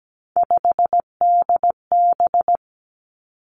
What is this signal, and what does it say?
Morse code, heard as a keyed tone, about 17 wpm, 700 Hz. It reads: 5DB
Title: Three Challenging Letter-Number Combinations 17wpm